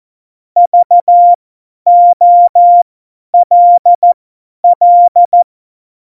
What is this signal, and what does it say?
Morse code, heard as a keyed tone, about 14 wpm, spaced as for 10 wpm, 700 Hz. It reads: VOLL
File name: Morsekurs / Witze / Witz 08. September 2020, 50 F70